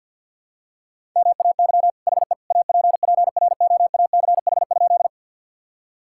Morse code, heard as a keyed tone, about 46 wpm, 700 Hz. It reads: MAY HE APPROACH?
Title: Sentences from Top 1000 Words - Review Part 6 - 50wpm